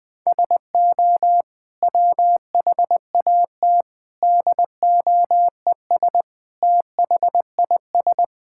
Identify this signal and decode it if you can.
Morse code, heard as a keyed tone, about 20 wpm, 700 Hz. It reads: SO WHAT DOES THIS
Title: Sets of 8 Words - 20wpm